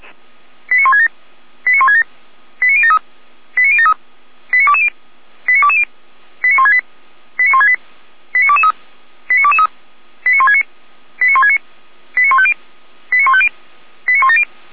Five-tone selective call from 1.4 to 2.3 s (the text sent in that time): 89167